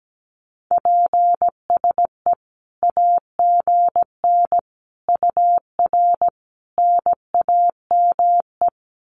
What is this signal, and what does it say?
Morse code, heard as a keyed tone, about 17 wpm, 700 Hz. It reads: PSE AGN UR NAME
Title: Common QSO Phrases 17wpm